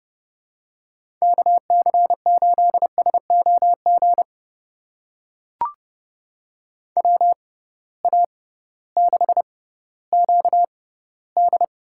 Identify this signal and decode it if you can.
Morse code, heard as a keyed tone, about 30 wpm, spaced as for 16 wpm, 700 Hz. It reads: KC8SOG WA6QD
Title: USA Call Signs - 6 Characters - 6x Character Spacing - 30wpm